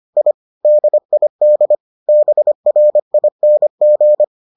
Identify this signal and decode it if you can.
Morse code, heard as a keyed tone, about 25 wpm, 600 Hz. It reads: I DID BRING